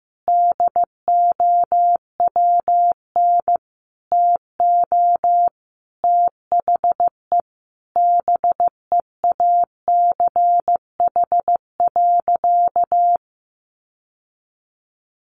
Morse code, heard as a keyed tone, about 15 wpm, 700 Hz. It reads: DOWN TO THE BEACH.
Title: Sentences from Top 1500 Words 15wpm